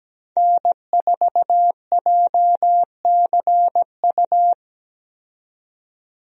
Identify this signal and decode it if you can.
Morse code, heard as a keyed tone, about 17 wpm, 700 Hz. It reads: N4JCU
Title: USA Call Signs - 5 Characters - 17wpm